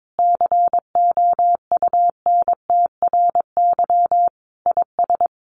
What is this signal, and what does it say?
Morse code, heard as a keyed tone, about 22 wpm, 700 Hz. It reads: COUNTRY IS
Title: Sentences from Top 1500 Words 22wpm